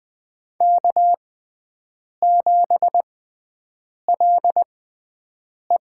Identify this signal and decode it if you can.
Morse code, heard as a keyed tone, about 20 wpm, spaced as for 7 wpm, 700 Hz. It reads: K7LE
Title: USA Call Signs - 6x Character Spacing - 20wpm